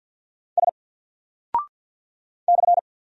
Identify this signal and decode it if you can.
Morse code, heard as a keyed tone, about 50 wpm, 700 Hz. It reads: S /
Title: Single Character (Letter:Number:Punctuation:Prosign) 50wpm